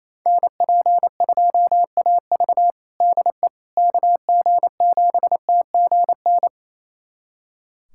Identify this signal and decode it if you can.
Morse code, heard as a keyed tone, about 28 wpm, 700 Hz. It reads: NP2AV DE KG7TGN
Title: USA Callsign DE Callsign 28wpm